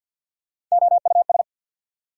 Morse code, heard as a keyed tone, about 53 wpm, 690 Hz. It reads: OUS
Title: Prefixes and Suffixes 50wpm